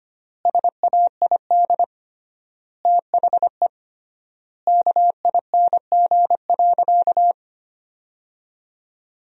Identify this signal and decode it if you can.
Morse code, heard as a keyed tone, about 25 wpm, 700 Hz. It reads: SAID THE KING.
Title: 3x Word-Spacing - Sentences from Top 400 Words - 25wpm